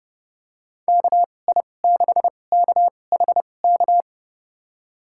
Morse code, heard as a keyed tone, about 29 wpm, spaced as for 21 wpm, 700 Hz. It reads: KI6KHK
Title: USA Call Signs - 6 Characters - 2x Character Spacing - 30wpm